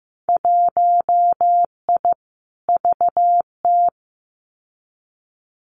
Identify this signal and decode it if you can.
Morse code, heard as a keyed tone, about 15 wpm, 700 Hz. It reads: 1I VT
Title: Winter Field Day Exchanges Encore 15wpm